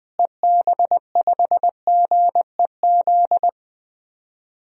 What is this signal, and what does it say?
Morse code, heard as a keyed tone, about 21 wpm, 700 Hz. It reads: EB5GEZ
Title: DXCC-100 Call Signs - 6 Characters - Repeated 2x - 20wpm